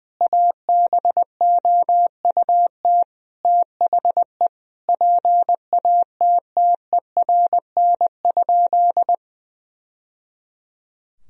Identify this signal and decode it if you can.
Morse code, heard as a keyed tone, about 20 wpm, 700 Hz. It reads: ABOUT THE PATTERN?
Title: Sentences from Top 1000 Words - Review Part 2 - 20wpm